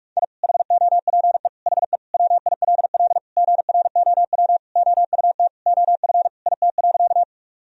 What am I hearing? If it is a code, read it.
Morse code, heard as a keyed tone, about 45 wpm, 700 Hz. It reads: I HOPE HE WILL GROW OUT OF IT.
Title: Sentences from Top 1000 Words - Review Part 1 - 45wpm